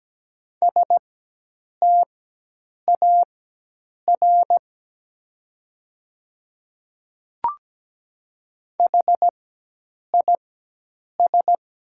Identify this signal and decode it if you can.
Morse code, heard as a keyed tone, about 17 wpm, spaced as for 8 wpm, 700 Hz. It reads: STAR HIS
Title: 4x Character Spacing - Top 500 Words - 17wpm